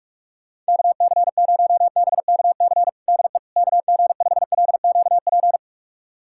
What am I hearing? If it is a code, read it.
Morse code, heard as a keyed tone, about 41 wpm, 700 Hz. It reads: KC0BKC DE KG5LQP
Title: USA Callsign DE Callsign - 6 Characters - 45wpm